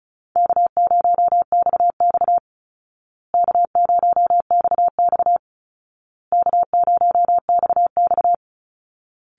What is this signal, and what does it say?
Morse code, heard as a keyed tone, about 34 wpm, 700 Hz. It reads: K0XX K0XX K0XX
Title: Easy Call Signs Repeated 3x - 35wpm